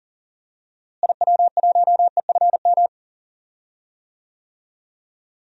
Morse code, heard as a keyed tone, about 40 wpm, 700 Hz. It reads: IW1EFM
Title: DXCC-100 Call Signs - 6 Characters - Repeated 2x - 40wpm